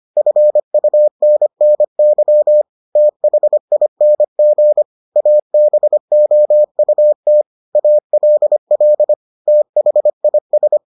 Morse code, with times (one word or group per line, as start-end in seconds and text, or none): 0.2-2.6 FUNNY
3.0-4.8 THING
5.2-7.4 ABOUT
7.8-9.1 ALL
9.5-10.8 THIS